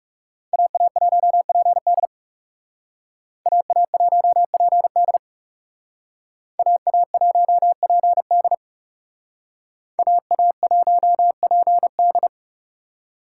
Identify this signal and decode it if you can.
Morse code, heard as a keyed tone, about 40 wpm, 700 Hz. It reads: AA1PD AA1PD AA1PD AA1PD